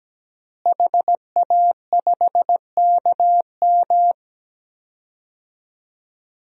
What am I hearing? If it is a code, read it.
Morse code, heard as a keyed tone, about 17 wpm, 700 Hz. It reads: HA5KM
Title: DXCC-100 Call Signs - 5 Characters - Repeated 2x - 17wpm